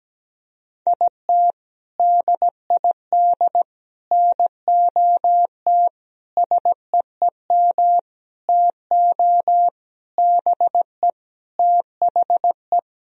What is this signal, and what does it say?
Morse code, heard as a keyed tone, about 17 wpm, 700 Hz. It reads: IT DID NOT SEEM TO BE THE
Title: Sentences from Top 1000 Words - Review Part 2 - 17wpm